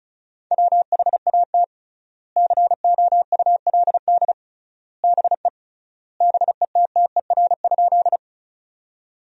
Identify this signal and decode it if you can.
Morse code, heard as a keyed tone, about 34 wpm, 700 Hz. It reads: WHAT COULD BE BETTER?
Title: 3x Word-Spacing - Sentences from Top 400 Words - 35wpm